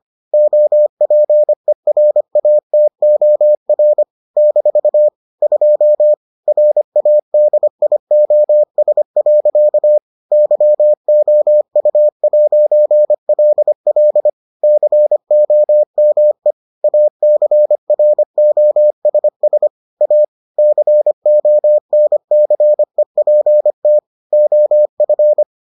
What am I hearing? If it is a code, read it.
Morse code, heard as a keyed tone, about 25 wpm, 600 Hz. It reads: OPERATOR - 2 RADIOS. YOU'LL COME ACROSS A CONCEPT OF